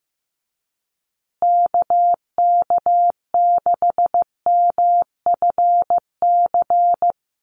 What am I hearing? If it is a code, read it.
Morse code, heard as a keyed tone, about 15 wpm, 700 Hz. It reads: KK6MFC
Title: USA Call Signs Repeated 2x 15wpm